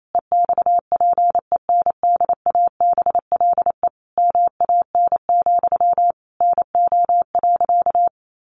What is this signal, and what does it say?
Morse code, heard as a keyed tone, about 28 wpm, 700 Hz. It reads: EXPENDABLE MAN, NO.
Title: A Sound Of Thunder 28wpm